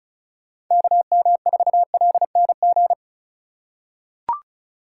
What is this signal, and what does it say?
Morse code, heard as a keyed tone, about 36 wpm, 700 Hz. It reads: KM4LNG
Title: USA Call Signs - 6 Characters - 2x Character Spacing - 35wpm